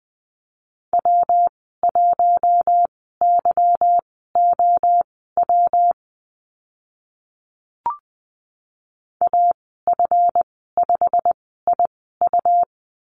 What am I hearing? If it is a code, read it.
Morse code, heard as a keyed tone, about 20 wpm, spaced as for 14 wpm, 700 Hz. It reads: W1YOW AF5IU
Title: 193 Lesson 31g - Letter v - Call Signs 20wpm